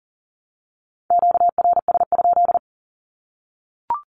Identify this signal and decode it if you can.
Morse code, heard as a keyed tone, about 40 wpm, 700 Hz. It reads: QRS?
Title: Q-Codes Repeated 3x 40wpm